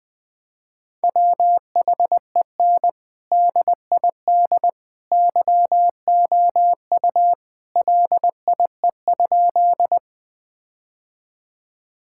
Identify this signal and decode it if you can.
Morse code, heard as a keyed tone, about 21 wpm, 700 Hz. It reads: WHEN DID YOU LIE?
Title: Sentences from Top 1600 Words 20wpm